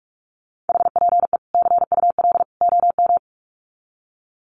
Morse code, heard as a keyed tone, about 44 wpm, 700 Hz. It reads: HPE CUL OM
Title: Common QSO Phrases 45wpm